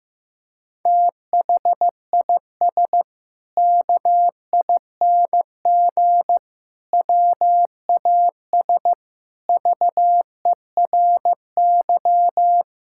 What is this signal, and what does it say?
Morse code, heard as a keyed tone, about 15 wpm, 700 Hz. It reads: THIS KING WAS VERY
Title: Sentences from Top 1600 Words 15wpm